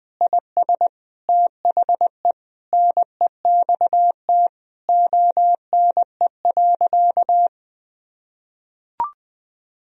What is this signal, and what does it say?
Morse code, heard as a keyed tone, about 20 wpm, 700 Hz. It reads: IS THE NEXT ONE.
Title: Sentences from Top 1000 Words - Review Part 1 - 20wpm